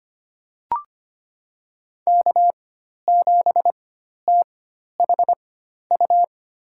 Morse code, heard as a keyed tone, about 25 wpm, spaced as for 12 wpm, 700 Hz. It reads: K7THU
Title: USA Call Signs - 4x Character Spacing - 25wpm